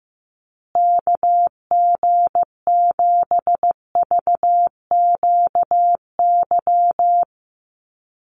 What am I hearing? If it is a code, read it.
Morse code, heard as a keyed tone, about 15 wpm, 700 Hz. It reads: KG7VQY
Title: USA Call Signs Repeated 3x 15wpm